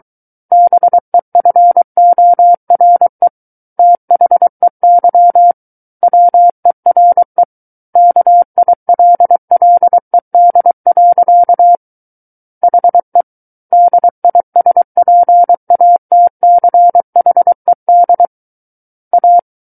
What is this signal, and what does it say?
Morse code, heard as a keyed tone, about 23 wpm, 700 Hz. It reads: BEFORE THEY WERE KILLED. HE DISPATCHED A
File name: Book One / 14-Chapter 14 - 23 WPM